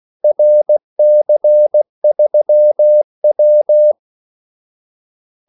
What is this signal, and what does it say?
Morse code, heard as a keyed tone, about 17 wpm, 600 Hz. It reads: RC3W
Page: Kaldesignaler 16 wpm